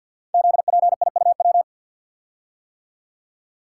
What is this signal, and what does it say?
Morse code, heard as a keyed tone, about 49 wpm, 700 Hz. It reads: ZPIUW
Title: Code Groups - 5 Letters - 50wpm